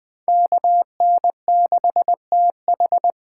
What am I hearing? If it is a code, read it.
Morse code, heard as a keyed tone, about 20 wpm, 700 Hz. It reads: KN6TH
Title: Speed Racing - USA 2x2 Call Signs 17wpm